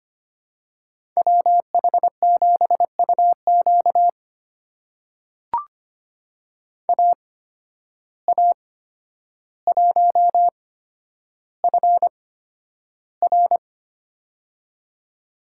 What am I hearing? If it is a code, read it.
Morse code, heard as a keyed tone, about 25 wpm, spaced as for 11 wpm, 700 Hz. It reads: WH7UQ AA1FR
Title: USA Call Signs - 5 Characters - 8x Character Spacing - 25wpm